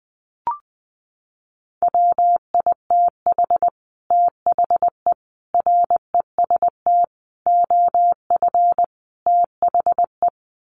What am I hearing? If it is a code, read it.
Morse code, heard as a keyed tone, about 20 wpm, 700 Hz. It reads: WITH THE REST OF THE
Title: Sets of 5 Words 20wpm